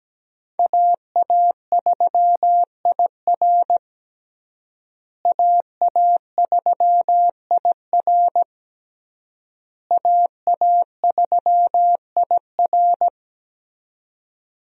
Morse code, heard as a keyed tone, about 17 wpm, 700 Hz. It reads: AA3IR AA3IR AA3IR
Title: USA Call Signs - 5 Characters - Repeated 3x - 17wpm